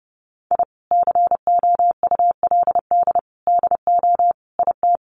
Morse code, heard as a keyed tone, about 30 wpm, 700 Hz. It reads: I COULD DO IT
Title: Sentences from Top 1300 Words 30wpm